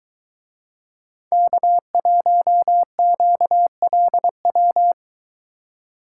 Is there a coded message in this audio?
Yes: Morse code from 1.3 to 4.9 s, heard as a keyed tone, about 23 wpm, 700 Hz. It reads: K1QLW